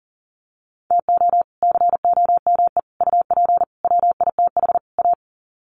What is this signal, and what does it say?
Morse code, heard as a keyed tone, about 40 wpm, 700 Hz. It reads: TO COME UP WITH A
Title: Sets of 5 Words 40wpm